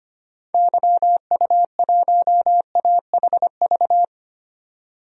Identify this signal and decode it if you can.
Morse code, heard as a keyed tone, about 25 wpm, 700 Hz. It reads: YU1AHV
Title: DXCC-100 Call Signs - 6 Characters - 25wpm